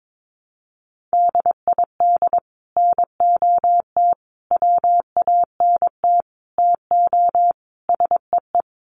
Morse code, heard as a keyed tone, about 22 wpm, 700 Hz. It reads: DID NOT WANT TO SEE
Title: Sets of 5 Words Encore2 22wpm